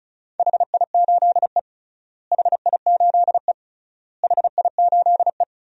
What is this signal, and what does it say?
Morse code, heard as a keyed tone, about 36 wpm, 700 Hz. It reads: HI8E HI8E HI8E
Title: DXCC-100 Call Signs - 4 Characters - Repeated 3x - 35wpm